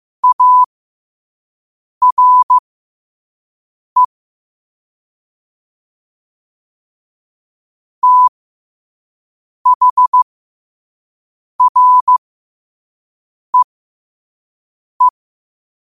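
Morse code, heard as a keyed tone, about 15 wpm, spaced as for 5 wpm, 1000 Hz. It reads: ARE THREE
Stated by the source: Morse quote-of-the-day practice at 5 wpm.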